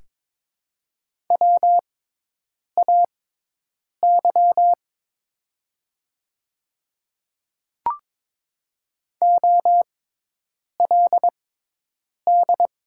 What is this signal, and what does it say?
Morse code, heard as a keyed tone, about 22 wpm, spaced as for 8 wpm, 700 Hz. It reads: WAY OLD
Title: 6x Character Spacing - Top 200 Words - 22wpm